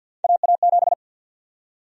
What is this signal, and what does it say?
Morse code, heard as a keyed tone, about 50 wpm, 700 Hz. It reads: AA7
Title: USA Call Sign Prefixes - 50wpm